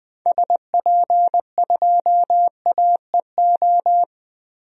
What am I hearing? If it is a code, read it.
Morse code, heard as a keyed tone, about 20 wpm, 700 Hz. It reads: SP2AEO
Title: DXCC-100 Call Signs - 6 Characters - 20wpm